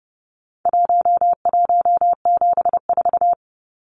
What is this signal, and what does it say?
Morse code, heard as a keyed tone, about 30 wpm, 700 Hz. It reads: 1174
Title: Numbers - Contest Style - 4-Digit 30wpm